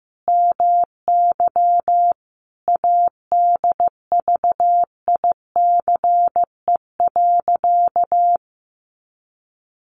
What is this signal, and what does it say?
Morse code, heard as a keyed tone, about 14 wpm, 700 Hz. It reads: MY ADVICE.
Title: Sentences from Top 1100 Words 15wpm